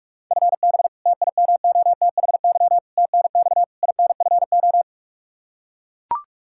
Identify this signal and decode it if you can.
Morse code, heard as a keyed tone, about 41 wpm, 700 Hz. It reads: FB TIMOTHY TNX INFO
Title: Practice QSO - Exchange 3 - More Info - 45wpm